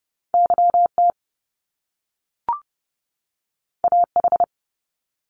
Morse code, heard as a keyed tone, about 30 wpm, 700 Hz. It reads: YT AH
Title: Top Two Letter Combinations in Words 30wpm